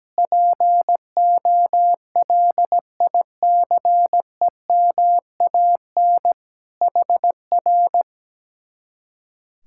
Morse code, heard as a keyed tone, about 17 wpm, 700 Hz. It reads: POLICEMAN HR
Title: Speed-Racing Common QSO Phrases - 17wpm